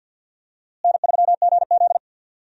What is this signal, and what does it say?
Morse code, heard as a keyed tone, about 46 wpm, 700 Hz. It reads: N3GZ
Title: USA Call Signs - 4 Characters - Repeated 3x - 50wpm